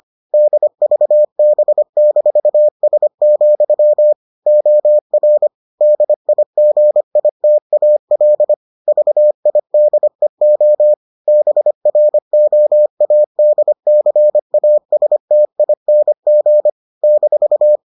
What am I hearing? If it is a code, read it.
Morse code, heard as a keyed tone, about 25 wpm, 600 Hz. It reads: DVB-S, OR DIGITAL VIDEO BROADCASTING -